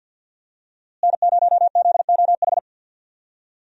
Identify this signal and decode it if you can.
Morse code, heard as a keyed tone, about 51 wpm, 700 Hz. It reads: N0ZOH
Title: USA Call Signs - 5 Characters - Repeated 2x - 50wpm